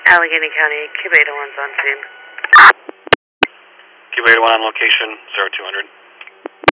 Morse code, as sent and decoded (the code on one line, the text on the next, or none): none